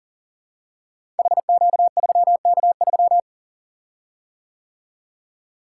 Morse code, heard as a keyed tone, about 42 wpm, 700 Hz. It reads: HQ3K3
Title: Code Groups - 5 Letter:Numbers - 40wpm